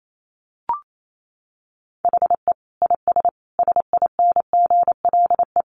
Morse code, heard as a keyed tone, about 28 wpm, 700 Hz. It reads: HE IS SINGLE
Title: Sentences from Top 700 Words 28wpm